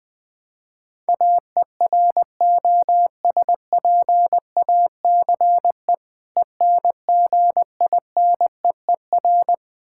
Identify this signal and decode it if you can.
Morse code, heard as a keyed tone, about 20 wpm, 700 Hz. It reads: AEROSPACE ENGINEER
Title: Practice QSO - Exchange 3 - More Info - 20wpm